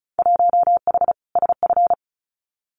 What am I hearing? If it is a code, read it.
Morse code, heard as a keyed tone, about 36 wpm, 700 Hz. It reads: 1H SF
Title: Winter Field Day Exchanges 35wpm